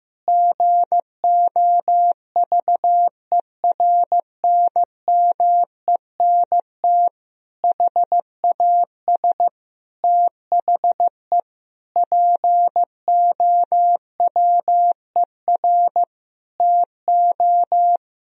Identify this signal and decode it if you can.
Morse code, heard as a keyed tone, about 15 wpm, 700 Hz. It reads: GOVERNMENT HAS THE POWER TO
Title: Sentences from Top 1000 Words - Review Part 3 - 15wpm